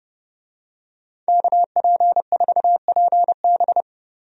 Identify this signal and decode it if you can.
Morse code, heard as a keyed tone, about 30 wpm, 700 Hz. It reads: KP4PB
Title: DXCC-100 Call Signs - 5 Characters - 30wpm